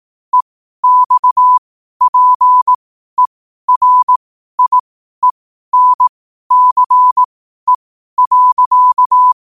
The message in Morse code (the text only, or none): EXPERIENCE.